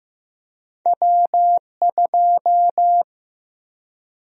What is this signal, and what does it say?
Morse code, heard as a keyed tone, about 15 wpm, 700 Hz. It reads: W2
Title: USA Call Sign Prefixes - 15wpm